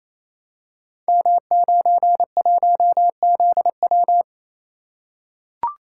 Morse code, heard as a keyed tone, about 27 wpm, 700 Hz. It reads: M91ZW
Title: Code Groups - 5 Letter:Numbers - 28wpm